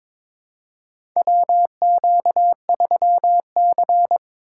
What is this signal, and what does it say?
Morse code, heard as a keyed tone, about 22 wpm, 700 Hz. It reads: WQ3C
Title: USA Call Signs - 4 Characters - 3x Character Spacing - 22wpm